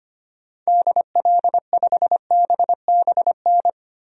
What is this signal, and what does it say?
Morse code, heard as a keyed tone, about 25 wpm, 700 Hz. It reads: DL5BBN
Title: DXCC-100 Call Signs - 6 Characters - 25wpm